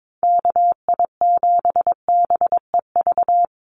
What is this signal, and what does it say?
Morse code, heard as a keyed tone, about 22 wpm, 700 Hz. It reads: KI7BEV
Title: USA Call Signs 22wpm